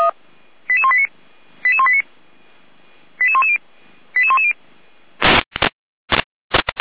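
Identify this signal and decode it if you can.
Five-tone selective call (ZVEI), heard as quick runs of five tones, tones of about 70 ms. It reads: 88189 88189 88110 88110
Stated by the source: ZVEI five-tone selective call received on SDR.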